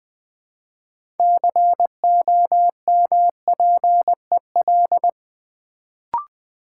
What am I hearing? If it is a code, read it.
Morse code, heard as a keyed tone, about 20 wpm, 690 Hz. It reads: COMPEL
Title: Build A Word - 3 Letters At A Time - 6-Letter Words - 20wpm